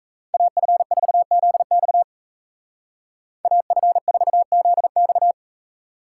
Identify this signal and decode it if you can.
Morse code, heard as a keyed tone, about 35 wpm, 700 Hz. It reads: AF4ZX AF4ZX
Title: Speed Racing - USA 2x2 Call Signs 28wpm